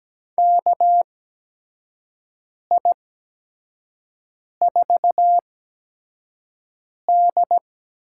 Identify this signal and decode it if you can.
Morse code, heard as a keyed tone, about 17 wpm, spaced as for 5 wpm, 700 Hz. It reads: KI4D
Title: USA Call Signs - 6 Characters - 8x Character Spacing - 17wpm